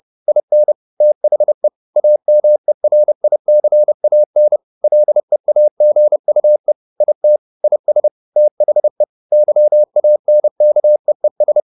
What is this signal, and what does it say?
Morse code, heard as a keyed tone, about 30 wpm, 600 Hz. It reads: IN THE AMERICAN LEAGUE IT IS THE YANKEES
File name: the old man and the sea / the old man and the sea 1 30wpm